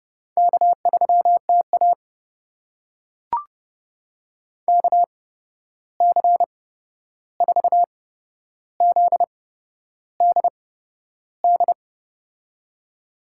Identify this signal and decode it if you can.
Morse code, heard as a keyed tone, about 32 wpm, spaced as for 11 wpm, 700 Hz. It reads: K3TA KC4ZDD
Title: USA Call Signs - 8x Character Spacing - 30wpm